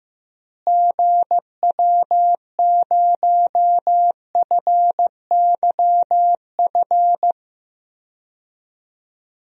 Morse code, heard as a keyed tone, about 15 wpm, 700 Hz. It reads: GW0FYF
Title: DXCC-100 Call Signs - 6 Characters - Repeated 3x - 15wpm